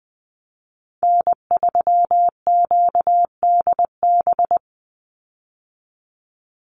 Morse code, heard as a keyed tone, about 20 wpm, 700 Hz. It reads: N3QDB
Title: USA Call Signs 20wpm